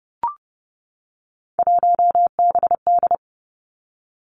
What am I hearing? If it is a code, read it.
Morse code, heard as a keyed tone, about 30 wpm, 700 Hz. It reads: 1BD
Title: Three Challenging Letter-Number Combinations 30wpm